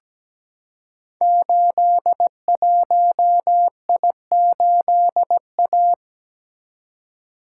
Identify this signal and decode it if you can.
Morse code, heard as a keyed tone, about 17 wpm, 700 Hz. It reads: 81I8A
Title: Code Groups - 5 Letter:Numbers - 17wpm